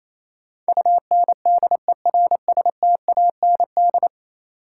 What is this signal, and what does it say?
Morse code, heard as a keyed tone, about 28 wpm, 700 Hz. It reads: UNDERSTAND
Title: Sets of 4 Words - New Word Drill - 28wpm